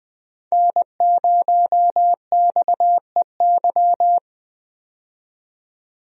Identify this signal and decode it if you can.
Morse code, heard as a keyed tone, about 20 wpm, 700 Hz. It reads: N0XEY
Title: USA Call Signs - 5 Characters - 20wpm